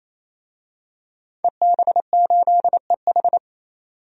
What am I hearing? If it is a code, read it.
Morse code, heard as a keyed tone, about 28 wpm, 700 Hz. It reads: EB8EH
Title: DXCC-100 Call Signs - 5 Characters - 28wpm